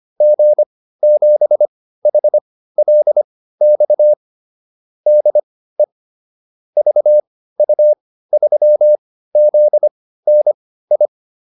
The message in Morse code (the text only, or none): G7HLX DE VU3ZNI